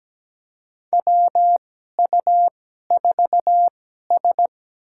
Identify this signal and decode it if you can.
Morse code, heard as a keyed tone, about 17 wpm, spaced as for 12 wpm, 700 Hz. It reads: WU4S
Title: USA Call Signs - 4 Characters - 2x Character Spacing - 17wpm